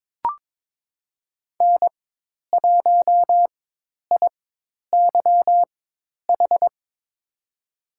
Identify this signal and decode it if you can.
Morse code, heard as a keyed tone, about 22 wpm, spaced as for 10 wpm, 700 Hz. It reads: N1IYH
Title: USA Call Signs - 4x Character Spacing - 22wpm